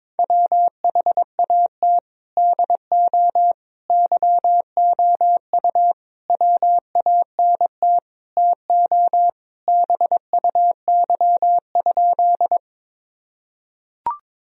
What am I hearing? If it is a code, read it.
Morse code, heard as a keyed tone, about 22 wpm, 700 Hz. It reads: WHAT DO YOU WANT TO BUY?
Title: Sentences from Top 1000 Words - Review Part 4 - 22wpm